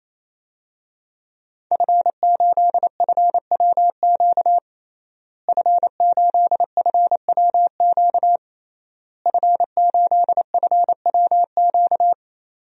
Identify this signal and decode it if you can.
Morse code, heard as a keyed tone, about 27 wpm, 700 Hz. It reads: F8FWQ F8FWQ F8FWQ
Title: DXCC-100 Call Signs - 5 Characters - Repeated 3x - 28wpm